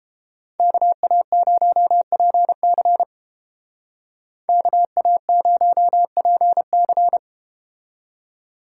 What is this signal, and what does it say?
Morse code, heard as a keyed tone, about 32 wpm, 700 Hz. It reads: KA0PC KA0PC